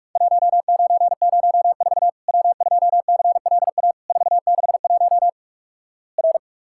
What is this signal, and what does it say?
Morse code, heard as a keyed tone, about 48 wpm, 700 Hz. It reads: 1904 W2CLA 461 R
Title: Worked All Europe DX Contest - QTC Exchange - 45wpm